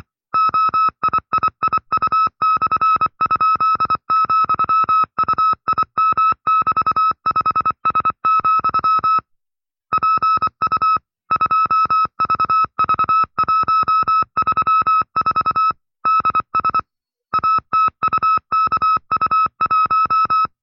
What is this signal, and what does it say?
Morse code, heard as a keyed tone, about 24 wpm, 1310 Hz. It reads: OIIIU/?,UIM<BT>5S, PU 2VV134 DS ATUKU1
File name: FreeFiles / CW-practice / 20WPM-side2